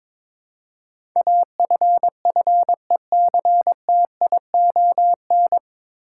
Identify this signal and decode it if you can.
Morse code, heard as a keyed tone, about 22 wpm, 700 Hz. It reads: AFFECTION
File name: 9 Letter Words - First 3 Letters Given - 22wpm